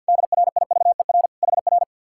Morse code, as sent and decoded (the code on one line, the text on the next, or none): -.. .-. .. ...- . .-. / .... .-.
DRIVER HR